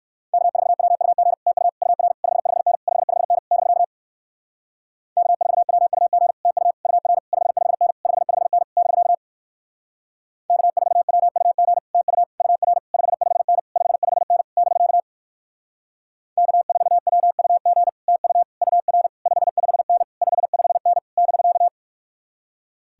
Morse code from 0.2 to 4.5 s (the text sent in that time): K4WUG TU UR 55N 55N <BK>